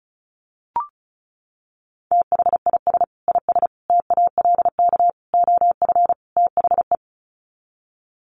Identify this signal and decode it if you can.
Morse code, heard as a keyed tone, about 34 wpm, 700 Hz. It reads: THIS IS TALK OF THE
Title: Sets of 5 Words 35wpm